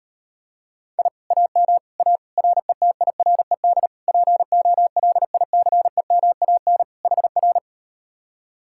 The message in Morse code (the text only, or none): I AM A RETIRED POLICEMAN HR